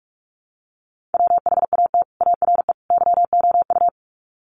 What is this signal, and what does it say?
Morse code, heard as a keyed tone, about 44 wpm, 700 Hz. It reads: WHAT ARE YOU
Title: Sets of 3 Words 45wpm